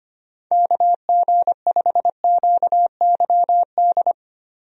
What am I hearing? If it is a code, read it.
Morse code, heard as a keyed tone, about 25 wpm, 700 Hz. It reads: KG5QYD
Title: USA Call Signs - 6 Characters - 25wpm